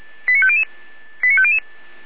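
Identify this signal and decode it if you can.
Five-tone selective call (ZVEI), heard as quick runs of five tones, tones of about 70 ms. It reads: 89500 89500
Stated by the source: Received via SDR; ZVEI five-tone selective call 88104.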